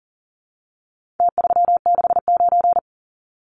Morse code, heard as a keyed tone, about 40 wpm, 700 Hz. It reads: T369
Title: Numbers - Contest Style - 4-Digit 40wpm